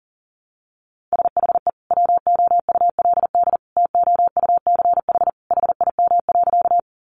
Morse code, heard as a keyed tone, about 40 wpm, 700 Hz. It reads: SHE WOULD TOUCH HIM.
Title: Sentences from Top 1200 Words 40wpm